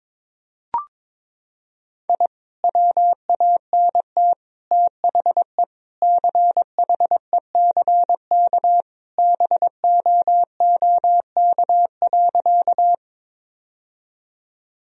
Morse code, heard as a keyed tone, about 22 wpm, 700 Hz. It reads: I WANT THE CHECK BOOK.
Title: Sentences from Top 1000 Words - Review Part 3 - 22wpm